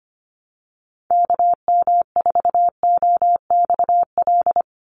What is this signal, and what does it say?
Morse code, heard as a keyed tone, about 25 wpm, 700 Hz. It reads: KM4OXL
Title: USA Call Signs 25wpm